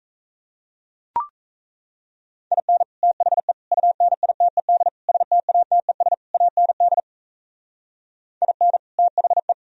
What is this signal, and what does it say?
Morse code, heard as a keyed tone, about 36 wpm, 700 Hz. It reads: IN THE UNITED STATES AND IN THE